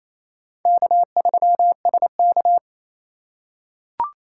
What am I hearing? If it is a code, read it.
Morse code, heard as a keyed tone, about 28 wpm, 700 Hz. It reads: K3SK
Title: USA Call Signs - 4 Characters - 2x Character Spacing - 28wpm